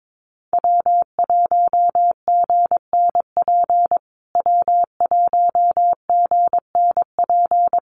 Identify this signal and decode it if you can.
Morse code, heard as a keyed tone, about 22 wpm, 700 Hz. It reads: W1GNP W1GNP
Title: USA Call Signs Repeated 3x 22wpm